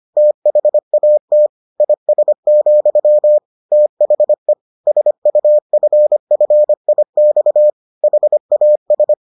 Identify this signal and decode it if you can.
Morse code, heard as a keyed tone, about 25 wpm, 600 Hz. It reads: THAT IS, THE SUFFIX HAS